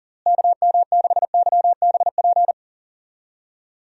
Morse code, heard as a keyed tone, about 40 wpm, 700 Hz. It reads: KM6YBP
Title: USA Call Signs - 6 Characters - Repeated 3x - 40wpm